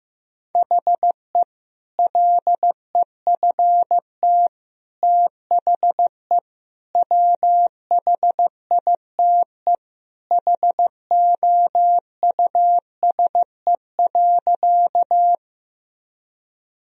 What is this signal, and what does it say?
Morse code, heard as a keyed tone, about 15 wpm, 700 Hz. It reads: HE LEFT THE WHITE HOUSE.
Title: Sentences from Top 1000 Words - Review Part 1 - 15wpm